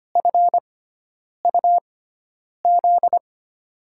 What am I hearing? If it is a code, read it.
Morse code, heard as a keyed tone, about 25 wpm, spaced as for 9 wpm, 700 Hz. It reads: FUZ
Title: USA Call Signs - 6 Characters - 6x Character Spacing - 25wpm